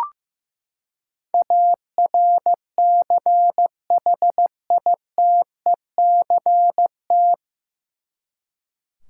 Morse code, heard as a keyed tone, about 15 wpm, 700 Hz. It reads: ARCHITECT
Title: Practice QSO - Exchange 3 - More Info - 15wpm